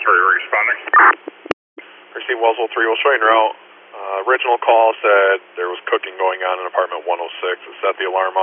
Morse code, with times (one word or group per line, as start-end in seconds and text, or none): none